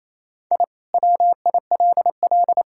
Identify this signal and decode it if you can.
Morse code, heard as a keyed tone, about 28 wpm, 700 Hz. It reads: I WILL
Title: Sentences from Top 1000 Words - Review Part 6 - 28wpm